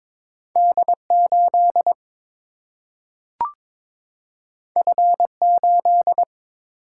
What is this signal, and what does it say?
Morse code, heard as a keyed tone, about 22 wpm, 700 Hz. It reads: D 8 F 8
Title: Single-Letter Single-Number 22wpm